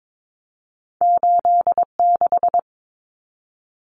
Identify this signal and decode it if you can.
Morse code, heard as a keyed tone, about 22 wpm, 700 Hz. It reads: 86
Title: Common Cut Numbers - 2-Digit 22wpm